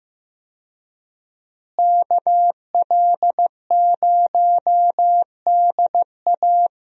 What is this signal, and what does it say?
Morse code, heard as a keyed tone, about 15 wpm, 700 Hz. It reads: KL0DA
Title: USA Call Signs - 5 Characters - Repeated 2x - 15wpm